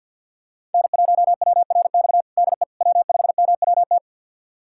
Morse code, heard as a keyed tone, about 47 wpm, 700 Hz. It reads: N1WRX DE W5MWT
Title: USA Callsign DE Callsign - 5 Characters - 50wpm